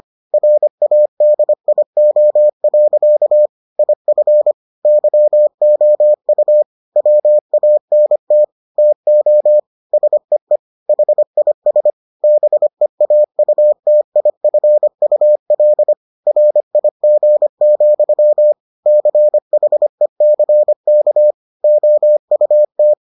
Morse code, heard as a keyed tone, about 25 wpm, 600 Hz. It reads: RADIO. IF YOU WANT TO SEE HIS BEAUTIFUL RIG, CHECK OUT